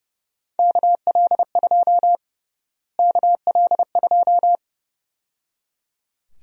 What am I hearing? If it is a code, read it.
Morse code, heard as a keyed tone, about 30 wpm, 700 Hz. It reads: KL2 KL2
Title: USA Call Sign Prefixes - Repeated 2x - 30wpm